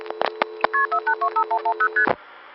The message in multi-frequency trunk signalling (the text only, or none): KP4825110ST